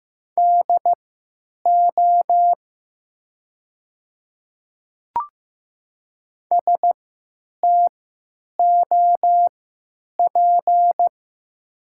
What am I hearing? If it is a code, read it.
Morse code, heard as a keyed tone, about 15 wpm, spaced as for 9 wpm, 700 Hz. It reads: DO STOP
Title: 3x Character Spacing - Top 300 Words - 15wpm